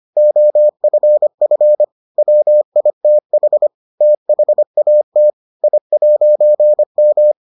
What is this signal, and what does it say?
Morse code, heard as a keyed tone, about 25 wpm, 600 Hz. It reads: OFF WITH THAT I'M